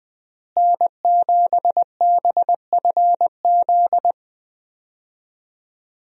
Morse code, heard as a keyed tone, about 21 wpm, 700 Hz. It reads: N7BFZ
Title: USA Call Signs - 5 Characters - Repeated 2x - 20wpm